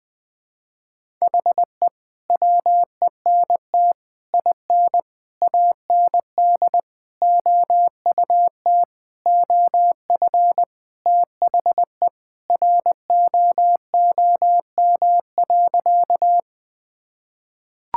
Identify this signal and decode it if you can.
Morse code, heard as a keyed tone, about 20 wpm, 700 Hz. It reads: HE WENT IN AND OUT OF THE ROOM.
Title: Sentences with 5-Word Phrases - 20wpm